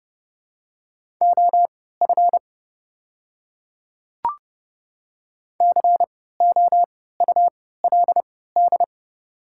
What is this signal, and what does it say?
Morse code, heard as a keyed tone, about 30 wpm, spaced as for 17 wpm, 700 Hz. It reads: OF COULD
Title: 3x Character Spacing - Top 100 Words - 30wpm